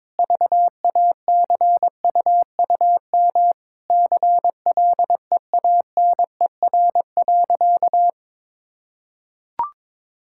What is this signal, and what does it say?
Morse code, heard as a keyed tone, about 22 wpm, 700 Hz. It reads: VACUUM CLEANER.
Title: Pangram Challenge v2 - 22wpm